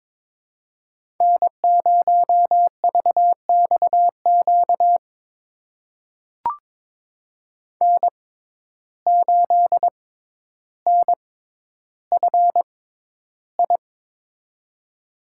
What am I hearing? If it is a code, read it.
Morse code, heard as a keyed tone, about 22 wpm, spaced as for 11 wpm, 700 Hz. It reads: N0VXQ N8NFI
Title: USA Call Signs - 5 Characters - 6x Character Spacing - 22wpm